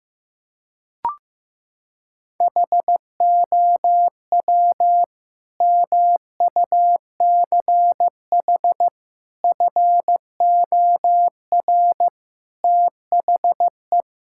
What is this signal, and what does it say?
Morse code, heard as a keyed tone, about 15 wpm, 700 Hz. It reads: HOW MUCH FOR THE
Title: Sentences from Top 1000 Words - Review Part 5 - 15wpm